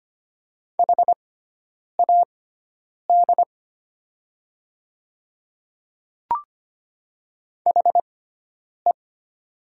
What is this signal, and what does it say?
Morse code, heard as a keyed tone, about 25 wpm, spaced as for 9 wpm, 700 Hz. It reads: HAD HE
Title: 6x Character Spacing - Top 100 Words - 25wpm